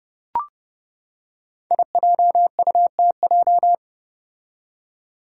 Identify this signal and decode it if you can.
Morse code, heard as a keyed tone, about 31 wpm, 700 Hz. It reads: IJUTJ
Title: Five Challenging Letter-Number Combinations 30wpm